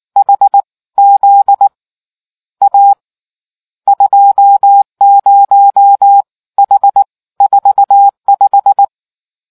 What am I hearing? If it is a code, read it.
Morse code, heard as a keyed tone, about 19 wpm, 800 Hz. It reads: HZ A 20 H 45